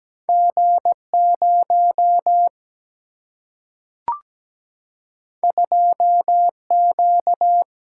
Morse code, heard as a keyed tone, about 17 wpm, 700 Hz. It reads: G 0 2 Q
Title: Single-Letter Single-Number 17wpm